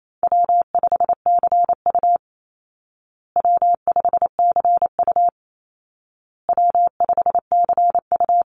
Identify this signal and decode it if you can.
Morse code, heard as a keyed tone, about 28 wpm, 700 Hz. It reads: W5CU W5CU W5CU
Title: Easy Call Signs Repeated 3x - 28wpm